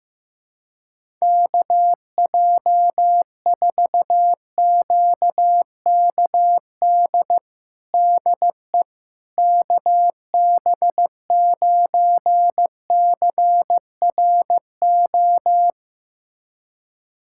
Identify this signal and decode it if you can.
Morse code, heard as a keyed tone, about 16 wpm, 700 Hz. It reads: KJ4QKD DE KB9CRO
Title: USA Callsign DE Callsign - 6 Characters - 15wpm